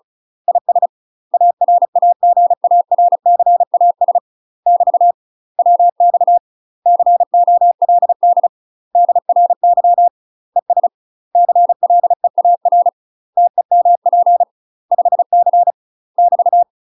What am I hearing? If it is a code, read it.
Morse code, heard as a keyed tone, about 35 wpm, 700 Hz. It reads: IS ARAGARCAS <BT> WX COLD DRY ES CLEAR TEMP 5C <BT>
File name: UK MASTER QSOs TRAINING / 35WPM / UK MASTER QSOs 0030